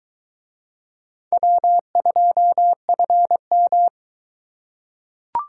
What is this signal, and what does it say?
Morse code, heard as a keyed tone, about 23 wpm, 700 Hz. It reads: W2FM